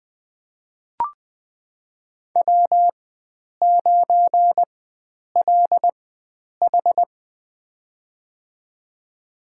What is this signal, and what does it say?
Morse code, heard as a keyed tone, about 20 wpm, spaced as for 9 wpm, 700 Hz. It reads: W9LH
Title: USA Call Signs - 4 Characters - 4x Character Spacing - 20wpm